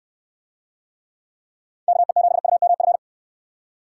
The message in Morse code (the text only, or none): KE7UMF